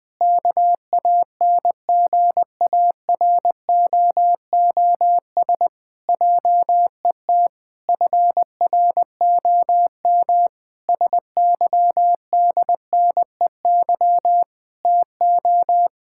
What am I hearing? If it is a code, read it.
Morse code, heard as a keyed tone, about 20 wpm, 700 Hz. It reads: KANGAROOS JET FROM SYDNEY TO
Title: Pangram Challenge v2 - 20wpm